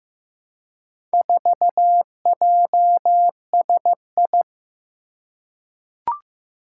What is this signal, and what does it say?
Morse code, heard as a keyed tone, about 15 wpm, 700 Hz. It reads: 4JSI
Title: Four Challenging Letter-Number Combinations 15wpm